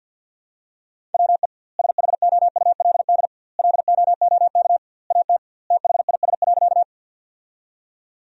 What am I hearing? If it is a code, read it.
Morse code, heard as a keyed tone, about 52 wpm, 700 Hz. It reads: WE SHOULD LOOK AT THIS.
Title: 2x Word-Spacing - Sentences from Top 300 Words - 50wpm